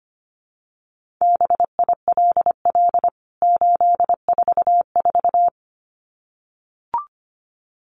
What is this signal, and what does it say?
Morse code, heard as a keyed tone, about 25 wpm, 700 Hz. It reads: BILL 844
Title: CWT Exchange 25wpm